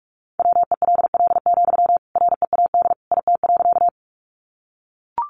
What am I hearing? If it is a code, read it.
Morse code, heard as a keyed tone, about 44 wpm, 700 Hz. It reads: WELL, READ IT.
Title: Sentences from Top 200 Words 45wpm